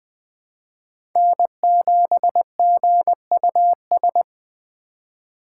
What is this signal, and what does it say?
Morse code, heard as a keyed tone, about 20 wpm, 700 Hz. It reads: N7GUS
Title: USA Call Signs - 5 Characters - 2x Character Spacing - 20wpm